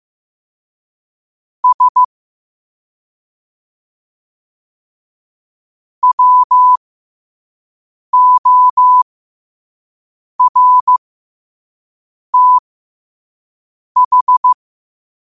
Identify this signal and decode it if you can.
Morse code, heard as a keyed tone, about 15 wpm, spaced as for 5 wpm, 1000 Hz. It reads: S WORTH
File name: qotdfiles / 5wpm